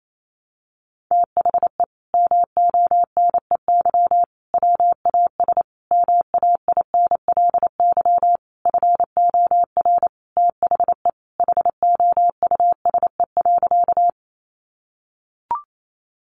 Morse code, heard as a keyed tone, about 28 wpm, 700 Hz. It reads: THE MONEY WAS MAINLY FOR THE HOUSE.
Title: Sentences from Top 1400 Words 28wpm